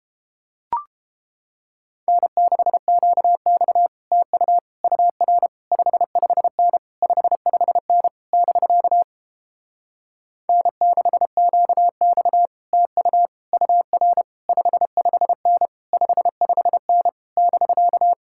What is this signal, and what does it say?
Morse code, heard as a keyed tone, about 31 wpm, 700 Hz. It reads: N6QX TU UR 55N 55N <BK> N6QX TU UR 55N 55N <BK>